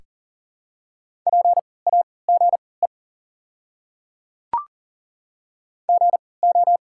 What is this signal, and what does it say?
Morse code, heard as a keyed tone, about 40 wpm, spaced as for 23 wpm, 700 Hz. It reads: PAGE GO
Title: 3x Character Spacing - Top 300 Words - 40wpm